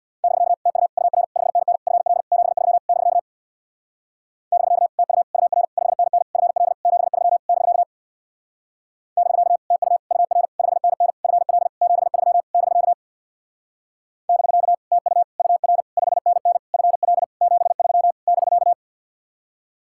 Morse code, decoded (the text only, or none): <BK> TU UR 5NN FL 73 <BK> <BK> TU UR 5NN FL 73 <BK> <BK> TU UR 5NN FL 73 <BK> <BK> TU UR 5NN FL 73 <BK>